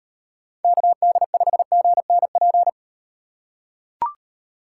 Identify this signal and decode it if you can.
Morse code, heard as a keyed tone, about 38 wpm, 700 Hz. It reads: KD5GNP